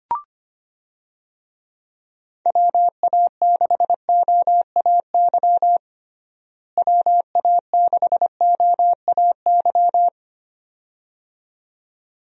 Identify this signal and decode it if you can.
Morse code, heard as a keyed tone, about 25 wpm, 700 Hz. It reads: WA6OAY WA6OAY
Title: USA Call Signs - 6 Characters - Repeated 2x - 25wpm